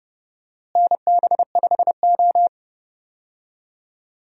Morse code, heard as a keyed tone, about 30 wpm, 700 Hz. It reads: NB5O
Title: USA Call Signs - 4 Characters - Repeated 2x - 30wpm